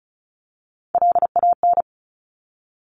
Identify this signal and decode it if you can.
Morse code, heard as a keyed tone, about 36 wpm, 700 Hz. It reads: LAN
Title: Top Three Letter Combinations in Words 35wpm